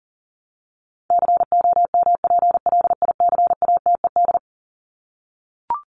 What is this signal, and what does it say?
Morse code, heard as a keyed tone, about 40 wpm, 700 Hz. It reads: COMPLICATED
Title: Top 1001-1250 Words 40wpm